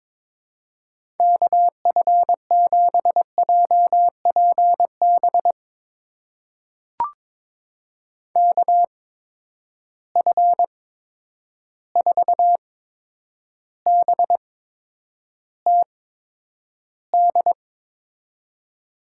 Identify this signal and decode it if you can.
Morse code, heard as a keyed tone, about 21 wpm, spaced as for 9 wpm, 700 Hz. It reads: KF7JPB KF4BTD
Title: USA Call Signs - 6 Characters - 8x Character Spacing - 22wpm